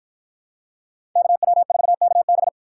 Morse code, heard as a keyed tone, about 50 wpm, 700 Hz. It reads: KW4KB